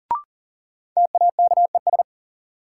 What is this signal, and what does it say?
Morse code, heard as a keyed tone, about 40 wpm, 700 Hz. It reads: TAKES
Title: Sentences from Top 1000 Words - Review Part 6 - New Word Drill - 40wpm